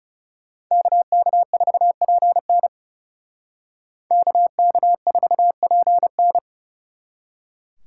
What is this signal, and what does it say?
Morse code, heard as a keyed tone, about 37 wpm, 700 Hz. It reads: KK4PN KK4PN
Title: Speed Racing - USA 2x2 Call Signs 30wpm